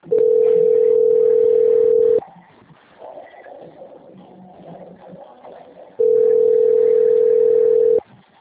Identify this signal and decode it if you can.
Ringback tone (North America), 440 + 480 Hz.